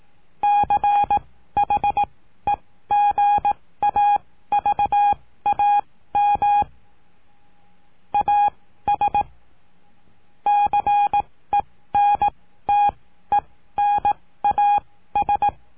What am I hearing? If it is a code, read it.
Morse code, heard as a keyed tone, about 18 wpm, spaced as for 14 wpm, 820 Hz. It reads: CHEGAVAM AS CENTENAS
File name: cursos / CURSO DE TELEGRAFIA / Telegrafia - Trilha 19